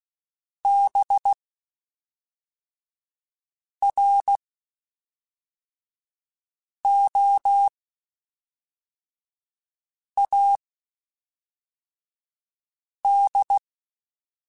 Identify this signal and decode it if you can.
Morse code, heard as a keyed tone, about 16 wpm, spaced as for 3 wpm, 800 Hz. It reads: BROAD
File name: Spaced / Chuckles Spaced-mp3s / MiddleAge sp @5